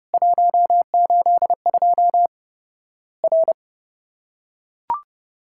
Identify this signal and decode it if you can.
Morse code, heard as a keyed tone, about 30 wpm, 700 Hz. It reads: 182 R
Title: Worked All Europe DX Contest - QTC Exchange - 30wpm